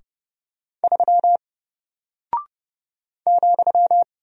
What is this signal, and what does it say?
Morse code, heard as a keyed tone, about 30 wpm, 700 Hz. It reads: 3 ,
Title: Single Character (Letter:Number:Punctuation:Prosign) Alt 30wpm